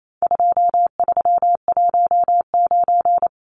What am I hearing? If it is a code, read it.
Morse code, heard as a keyed tone, about 28 wpm, 700 Hz. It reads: 2319
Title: Numbers - 4-Digit 28wpm